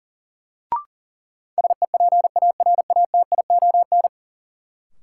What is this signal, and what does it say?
Morse code, heard as a keyed tone, about 37 wpm, 700 Hz. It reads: SEPARATION
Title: Sets of 5 Words - New Word Drill - 40wpm